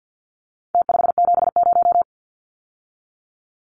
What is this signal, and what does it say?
Morse code, heard as a keyed tone, about 50 wpm, 700 Hz. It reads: T570
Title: Numbers - Contest Style - 4-Digit 50wpm